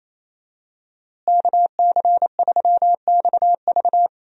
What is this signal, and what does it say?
Morse code, heard as a keyed tone, about 27 wpm, 700 Hz. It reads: KC3XV
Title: USA Call Signs - 5 Characters - Repeated 2x - 28wpm